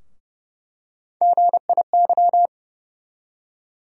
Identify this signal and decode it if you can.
Morse code, heard as a keyed tone, about 30 wpm, 700 Hz. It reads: GIY
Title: USA Call Sign Postfixes - 3 Letters 30wpm